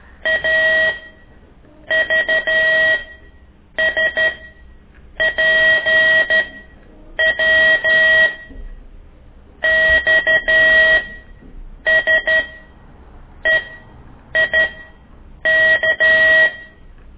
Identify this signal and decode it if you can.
Morse code, heard as a keyed tone, about 12 wpm, spaced as for 7 wpm, 1900 Hz. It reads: AVSPW XSEIK
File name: ik1weg2 / CORSI CW